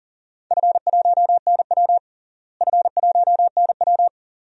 Morse code, heard as a keyed tone, about 40 wpm, 700 Hz. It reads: F1NW F1NW
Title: DXCC-100 Call Signs - 4 Characters - Repeated 2x - 40wpm